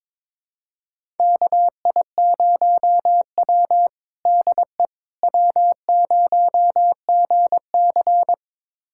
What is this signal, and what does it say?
Morse code, heard as a keyed tone, about 22 wpm, 700 Hz. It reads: KI0W DE W0GC
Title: USA Callsign DE Callsign - 4 Characters - 22wpm